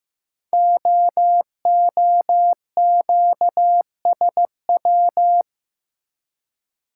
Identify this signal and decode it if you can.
Morse code, heard as a keyed tone, about 15 wpm, 700 Hz. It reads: OOQSW
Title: Code Groups - 5 Letters - 15wpm